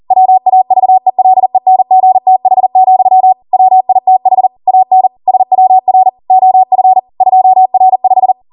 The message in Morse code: WAVELENGTH, WITH AN SWR OF 2R5